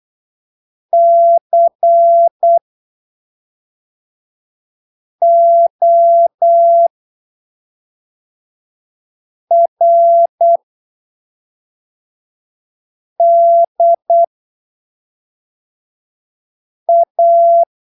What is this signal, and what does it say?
Morse code, heard as a keyed tone, about 8 wpm, spaced as for 3 wpm, 680 Hz. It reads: CORDA
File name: arquivos / texto / texto 29 08ppm